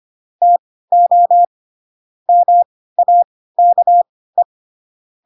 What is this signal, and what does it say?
Morse code, heard as a keyed tone, about 25 wpm, spaced as for 16 wpm, 700 Hz. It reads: TO MAKE
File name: Pinocchio / Pinocchio ch1 10 25 ik1hge